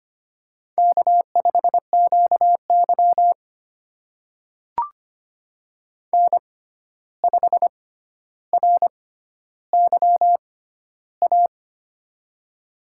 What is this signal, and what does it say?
Morse code, heard as a keyed tone, about 25 wpm, spaced as for 12 wpm, 700 Hz. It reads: K5QY N5RYA
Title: USA Call Signs - 6x Character Spacing - 25wpm